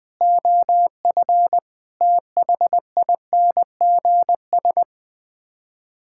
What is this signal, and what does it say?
Morse code, heard as a keyed tone, about 21 wpm, 700 Hz. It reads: OF THINGS
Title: Sets of 6 Words - 20wpm